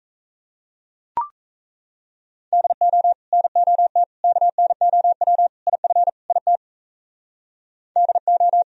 Morse code, heard as a keyed tone, about 34 wpm, 700 Hz. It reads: DO NOT KNOW IF IT DO
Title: Speed-Racing - Sets of 5 Words 28wpm